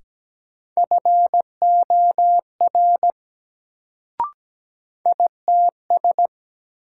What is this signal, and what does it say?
Morse code, heard as a keyed tone, about 17 wpm, 700 Hz. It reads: FOR ITS
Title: Sets of 2 Words - New Word Drill - 17wpm